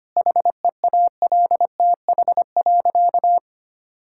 Morse code, heard as a keyed tone, about 25 wpm, 700 Hz. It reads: HEALTH.
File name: practice / 25wpm / Difficult Sentences with Hints - 25wpm